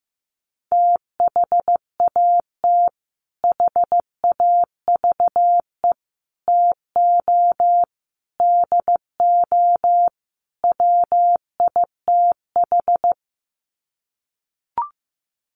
Morse code, heard as a keyed tone, about 15 wpm, 700 Hz. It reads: THAT HAVE TO DO WITH
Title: Sets of 5 Words Encore2 15wpm